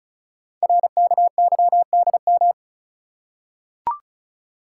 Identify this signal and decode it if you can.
Morse code, heard as a keyed tone, about 36 wpm, 700 Hz. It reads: RKYDM
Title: Code Groups - 5 Letters - 35wpm